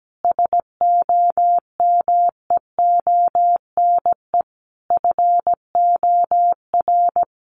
Morse code, heard as a keyed tone, about 17 wpm, 700 Hz. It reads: SOMEONE FOR
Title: Sentences from Top 600 Words 17wpm